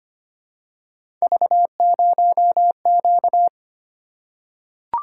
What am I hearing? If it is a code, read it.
Morse code, heard as a keyed tone, about 25 wpm, 700 Hz. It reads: V0Q
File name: Three Letter-Number Combinations - 25wpm